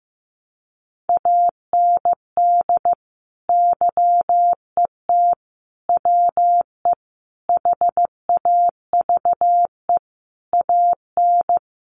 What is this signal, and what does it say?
Morse code, heard as a keyed tone, about 15 wpm, 700 Hz. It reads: AND YET WE HAVE AN
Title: Sentences from Top 1500 Words 15wpm